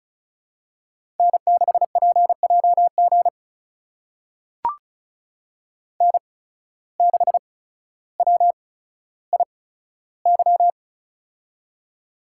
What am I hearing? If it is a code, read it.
Morse code, heard as a keyed tone, about 36 wpm, spaced as for 15 wpm, 700 Hz. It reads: N6PJG N6WIY
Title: USA Call Signs - 5 Characters - 8x Character Spacing - 35wpm